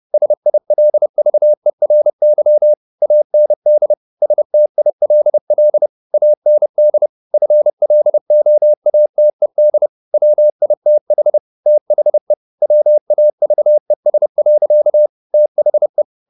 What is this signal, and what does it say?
Morse code, heard as a keyed tone, about 30 wpm, 600 Hz. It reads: SILVERY AND STILL AND FLOATED WITH THE WAVES. THE